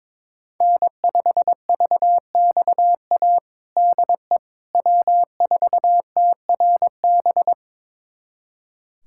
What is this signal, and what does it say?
Morse code, heard as a keyed tone, about 22 wpm, 700 Hz. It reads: N5VXA DE W4TRB
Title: USA Callsign DE Callsign - 5 Characters - 22wpm